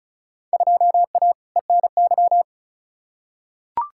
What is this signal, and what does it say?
Morse code, heard as a keyed tone, about 36 wpm, 700 Hz. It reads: 2A ENY
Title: Field Day Exchanges - 35wpm